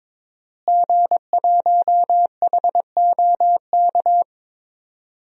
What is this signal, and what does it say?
Morse code, heard as a keyed tone, about 22 wpm, 700 Hz. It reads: G1HOK
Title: DXCC-100 Call Signs - 5 Characters - Repeated 3x - 22wpm